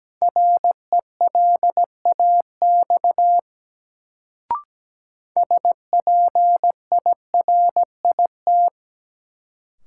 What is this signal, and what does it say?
Morse code, heard as a keyed tone, about 17 wpm, 700 Hz. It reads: RELAX SPIRIT
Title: Sentences from Top 1400 Words - New Word Drill - 17wpm